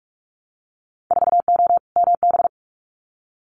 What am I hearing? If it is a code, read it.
Morse code, heard as a keyed tone, about 44 wpm, 700 Hz. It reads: 4O MB
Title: Winter Field Day Exchanges Encore 45wpm